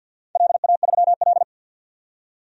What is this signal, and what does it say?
Morse code, heard as a keyed tone, about 51 wpm, 700 Hz. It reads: LA3L
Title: DXCC-100 Call Signs - 4 Characters - 50wpm